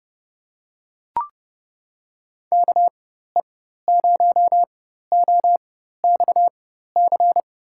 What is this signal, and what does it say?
Morse code, heard as a keyed tone, about 30 wpm, spaced as for 14 wpm, 700 Hz. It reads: KE0OXC
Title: USA Call Signs - 6 Characters - 4x Character Spacing - 30wpm